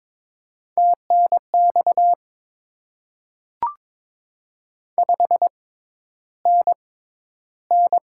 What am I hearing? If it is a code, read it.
Morse code, heard as a keyed tone, about 22 wpm, spaced as for 11 wpm, 690 Hz. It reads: TNX 5NN
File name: Top 20 QSO Elements - 6x Character Spacing - 22wpm